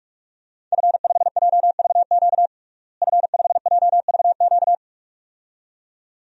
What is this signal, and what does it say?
Morse code, heard as a keyed tone, about 46 wpm, 700 Hz. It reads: F5JVQ F5JVQ
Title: DXCC-100 Call Signs - 5 Characters - Repeated 2x - 45wpm